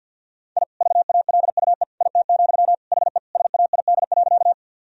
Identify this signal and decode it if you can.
Morse code, heard as a keyed tone, about 52 wpm, 700 Hz. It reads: I VALUE IT, HE SAID.
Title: Sentences from Top 1000 Words - Review Part 3 - 50wpm